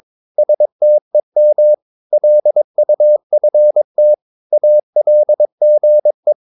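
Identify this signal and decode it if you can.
Morse code, heard as a keyed tone, about 22 wpm, 600 Hz. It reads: STEM LUFT ALGE
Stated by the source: Morse practice of short Danish words at 22 wpm, from CW med Gnister.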